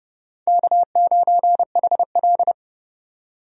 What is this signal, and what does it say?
Morse code, heard as a keyed tone, about 30 wpm, 700 Hz. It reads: K9HL
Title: USA Call Signs - 4 Characters - 30wpm